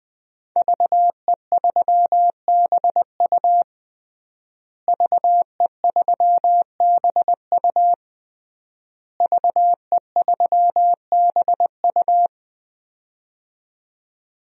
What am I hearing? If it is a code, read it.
Morse code, heard as a keyed tone, about 20 wpm, 700 Hz. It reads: VE3BU VE3BU VE3BU
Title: DXCC-100 Call Signs - 5 Characters - Repeated 3x - 20wpm